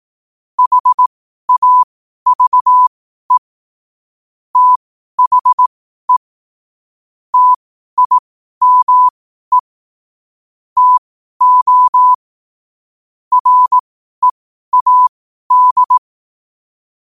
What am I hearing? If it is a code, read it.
Morse code, heard as a keyed tone, about 18 wpm, spaced as for 12 wpm, 1000 Hz. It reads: HAVE THE TIME TO READ
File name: qotdfiles / 12wpm